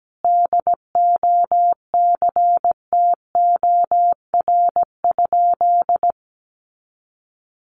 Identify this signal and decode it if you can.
Morse code, heard as a keyed tone, about 17 wpm, 700 Hz. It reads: DOCTOR?
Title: Sentences from Top 1500 Words 17wpm